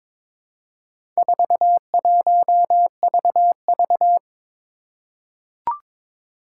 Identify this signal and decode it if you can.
Morse code, heard as a keyed tone, about 22 wpm, 700 Hz. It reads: 41VV
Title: Four Challenging Letter-Number Combinations 22wpm